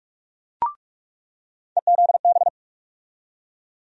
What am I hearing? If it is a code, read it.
Morse code, heard as a keyed tone, about 47 wpm, 700 Hz. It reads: EZB
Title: USA Call Sign Postfixes - 3 Letters 45wpm